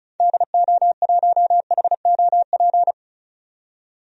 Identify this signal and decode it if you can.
Morse code, heard as a keyed tone, about 34 wpm, 700 Hz. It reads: DO1HOP
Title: DXCC-100 Call Signs - 6 Characters - 35wpm